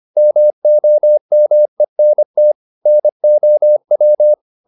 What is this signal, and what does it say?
Morse code, heard as a keyed tone, about 25 wpm, 600 Hz. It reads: MOMENT NOW
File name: the old man and the sea part 1 25wpm